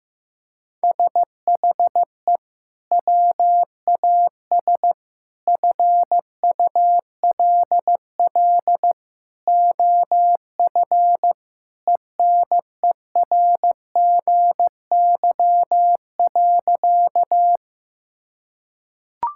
Morse code, heard as a keyed tone, about 15 wpm, 700 Hz. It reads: SHE WAS FULL OF ENERGY.
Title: Sentences from Top 1000 Words - Review Part 6 - 15wpm